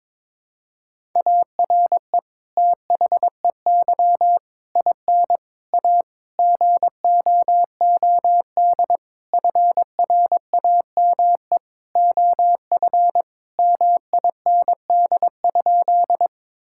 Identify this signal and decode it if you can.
Morse code, heard as a keyed tone, about 22 wpm, 700 Hz. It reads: ARE THEY IN A GOOD FRAME OF MIND?
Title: Sentences from Top 1000 Words - Review Part 6 - 22wpm